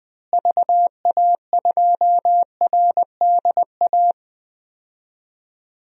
Morse code, heard as a keyed tone, about 21 wpm, 700 Hz. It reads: VA2RDA
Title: DXCC-100 Call Signs - 6 Characters - Repeated 3x - 20wpm